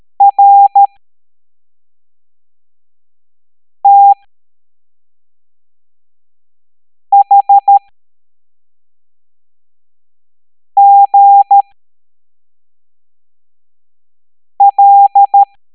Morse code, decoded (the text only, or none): RTHGL